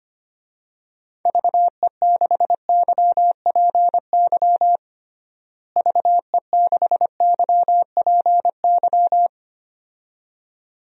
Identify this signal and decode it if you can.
Morse code, heard as a keyed tone, about 25 wpm, 700 Hz. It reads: VE6YPY VE6YPY
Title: DXCC-100 Call Signs - 6 Characters - Repeated 2x - 25wpm